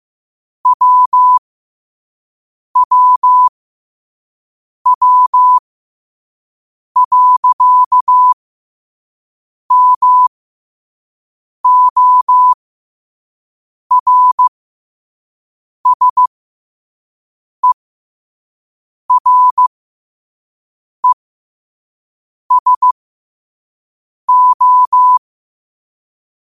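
Morse code, heard as a keyed tone, about 15 wpm, spaced as for 5 wpm, 1000 Hz. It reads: WWW.MORSERESO
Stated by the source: Morse code quote of the day at 5 wpm.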